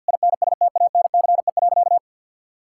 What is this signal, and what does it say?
Morse code, heard as a keyed tone, about 52 wpm, 700 Hz. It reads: INSTANCE.
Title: Sentences from Top 1000 Words - Review Part 5 - 50wpm